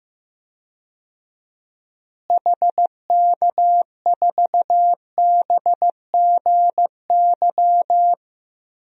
Morse code, heard as a keyed tone, about 15 wpm, 700 Hz. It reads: HK4BGY
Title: DXCC-100 Call Signs - 6 Characters - Repeated 3x - 15wpm